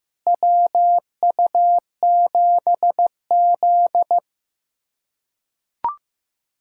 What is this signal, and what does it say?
Morse code, heard as a keyed tone, about 15 wpm, 700 Hz. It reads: WU7Z
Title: USA Callsign DE Callsign - 4 Characters - 15wpm